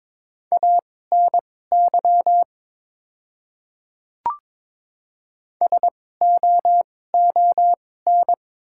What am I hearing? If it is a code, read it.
Morse code, heard as a keyed tone, about 23 wpm, spaced as for 16 wpm, 690 Hz. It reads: ANY SOON
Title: 2x Character Spacing - Top 500 Words - 22wpm